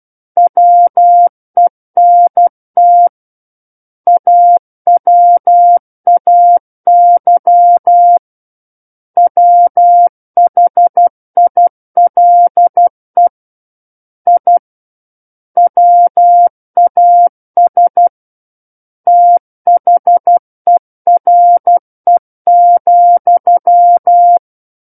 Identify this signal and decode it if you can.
Morse code, heard as a keyed tone, about 12 wpm, 700 Hz. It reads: WENT AWAY WHILE I WAS THERE,
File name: Book One / 03-Chapter 3 - 12 WPM